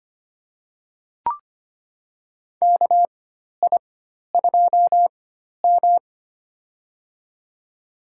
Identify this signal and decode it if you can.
Morse code, heard as a keyed tone, about 25 wpm, spaced as for 12 wpm, 690 Hz. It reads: KI2M
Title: USA Call Signs - 4 Characters - 4x Character Spacing - 25wpm